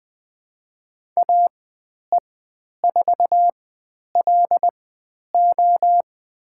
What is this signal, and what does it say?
Morse code, heard as a keyed tone, about 20 wpm, spaced as for 10 wpm, 700 Hz. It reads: AE4LO
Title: 109 Lesson 18g - Letter m - Call Signs 20wpm